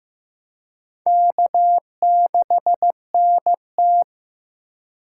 Morse code, heard as a keyed tone, about 15 wpm, 700 Hz. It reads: K6NT
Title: USA Call Signs - 4 Characters - 3x Character Spacing - 15wpm